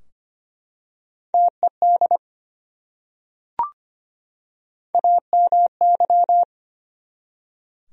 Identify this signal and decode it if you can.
Morse code, heard as a keyed tone, about 25 wpm, 700 Hz. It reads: TED AMY
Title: USA Common Names - Max 3 Letters - 25wpm